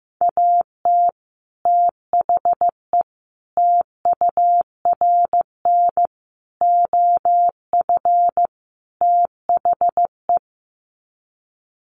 Morse code, heard as a keyed tone, about 15 wpm, 700 Hz. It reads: AT THE TURN OF THE
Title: Sets of 5 Words 15wpm